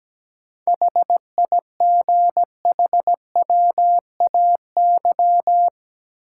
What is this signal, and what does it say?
Morse code, heard as a keyed tone, about 17 wpm, 700 Hz. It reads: HIGHWAY
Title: Sentences from Top 1400 Words - New Word Drill - 17wpm